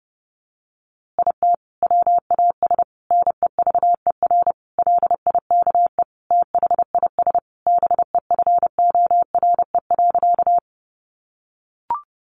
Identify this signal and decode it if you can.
Morse code, heard as a keyed tone, about 30 wpm, 700 Hz. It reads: IT WAS NEVER LIKE THIS BEFORE.
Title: A Sound Of Thunder 30wpm